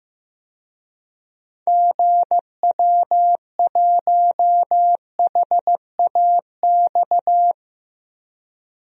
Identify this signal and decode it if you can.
Morse code, heard as a keyed tone, about 16 wpm, 700 Hz. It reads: GW1HAX
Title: DXCC-100 Call Signs - 6 Characters - Repeated 3x - 15wpm